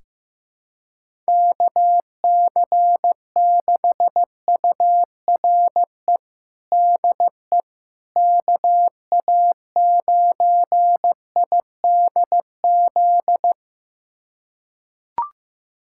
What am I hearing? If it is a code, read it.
Morse code, heard as a keyed tone, about 16 wpm, 700 Hz. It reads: KC6URE DE KA9IDZ
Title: USA Callsign DE Callsign - 6 Characters - 15wpm